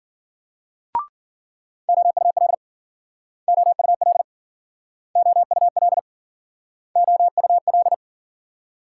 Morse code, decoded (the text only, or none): OUL OUL OUL OUL